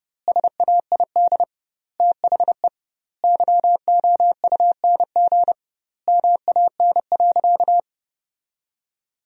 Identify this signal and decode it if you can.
Morse code, heard as a keyed tone, about 32 wpm, 700 Hz. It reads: SAID THE YOUNG MAN.
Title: 2x Word-Spacing - Sentences from Top 400 Words - 30wpm